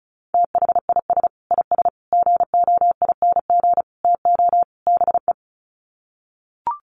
Morse code, heard as a keyed tone, about 34 wpm, 700 Hz. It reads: THIS IS GOING TO BE
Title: Sets of 5 Words 35wpm